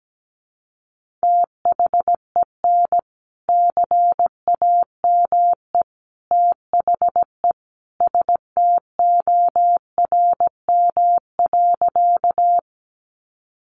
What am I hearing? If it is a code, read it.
Morse code, heard as a keyed tone, about 17 wpm, 700 Hz. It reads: THEN CAME THE STORM.
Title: Sentences from Top 1500 Words 17wpm